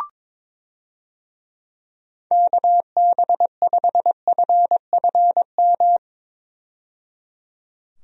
Morse code, heard as a keyed tone, about 22 wpm, 700 Hz. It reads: KB5FFM
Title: USA Call Signs - 6 Characters - 22wpm